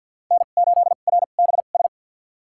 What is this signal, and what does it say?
Morse code, heard as a keyed tone, about 48 wpm, spaced as for 35 wpm, 700 Hz. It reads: N8RBS
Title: USA Call Signs - 5 Characters - 2x Character Spacing - 50wpm